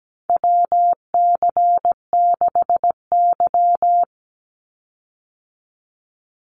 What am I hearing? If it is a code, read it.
Morse code, heard as a keyed tone, about 17 wpm, 700 Hz. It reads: WC6Y
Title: Call Signs Repeated 2x - 17wpm